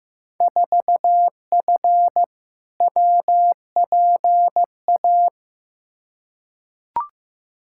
Morse code, heard as a keyed tone, about 15 wpm, 700 Hz. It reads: 4F WPA
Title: Field Day Exchanges - 15wpm